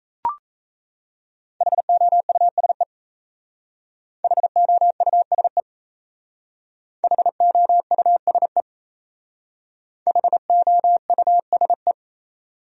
Morse code, heard as a keyed tone, about 35 wpm, 700 Hz. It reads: HOUSE HOUSE HOUSE HOUSE
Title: Speed-Racing Top 200 Words 28wpm